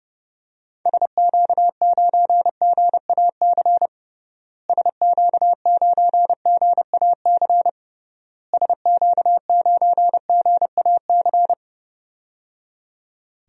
Morse code, heard as a keyed tone, about 30 wpm, 700 Hz. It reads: SQ9GAC SQ9GAC SQ9GAC
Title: DXCC-100 Call Signs - 6 Characters - Repeated 3x - 30wpm